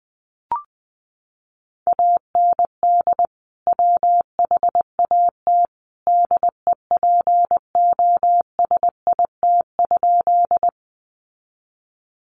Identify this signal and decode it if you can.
Morse code, heard as a keyed tone, about 20 wpm, 700 Hz. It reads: AND WHAT DEPOSIT?
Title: Sentences from Top 1300 Words 20wpm